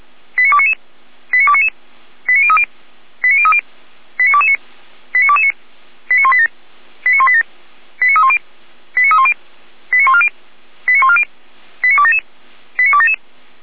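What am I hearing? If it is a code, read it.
Five-tone selective call (ZVEI), heard as quick runs of five tones, tones of about 70 ms. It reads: 89299 89299 89030 89030 89229 89229 89187 89187 89310 89310 89140 89140 89277 89277
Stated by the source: ZVEI five-tone call received on SDR, from the Feuerwehr Orlamünde fire brigade server.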